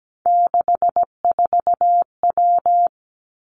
Morse code, heard as a keyed tone, about 17 wpm, 700 Hz. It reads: 64W
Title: Three Challenging Letter-Number Combinations 17wpm